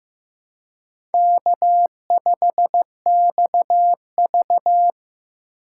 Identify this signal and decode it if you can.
Morse code, heard as a keyed tone, about 15 wpm, 700 Hz. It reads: K5XV
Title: USA Call Signs - 4 Characters - 3x Character Spacing - 15wpm